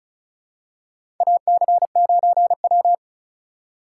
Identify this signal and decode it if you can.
Morse code, heard as a keyed tone, about 34 wpm, 700 Hz. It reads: AC9W
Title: USA Call Signs - 4 Characters - 4x Character Spacing - 35wpm